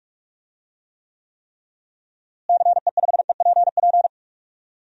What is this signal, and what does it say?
Morse code, heard as a keyed tone, about 46 wpm, 700 Hz. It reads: CE5EPP
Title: DXCC-100 Call Signs - 6 Characters - Repeated 3x - 45wpm